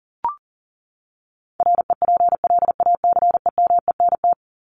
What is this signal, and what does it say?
Morse code, heard as a keyed tone, about 40 wpm, 700 Hz. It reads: REPLACEMENT
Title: Top 1751-2000 Words 40wpm